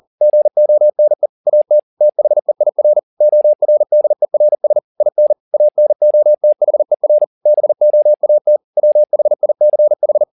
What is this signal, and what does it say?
Morse code, heard as a keyed tone, about 40 wpm, 600 Hz. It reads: GONE AT THEIR ORDERS IN ANOTHER BOAT WHICH
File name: the old man and the sea 1 40wpm